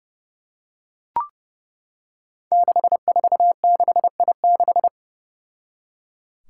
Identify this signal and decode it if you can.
Morse code, heard as a keyed tone, about 28 wpm, 700 Hz. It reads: 646I6
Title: Five Challenging Letter-Number Combinations 30wpm